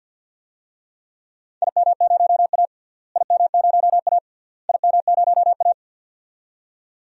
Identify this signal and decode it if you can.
Morse code, heard as a keyed tone, about 48 wpm, 700 Hz. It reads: IM0A IM0A IM0A
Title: DXCC-100 Call Signs - 4 Characters - Repeated 3x - 50wpm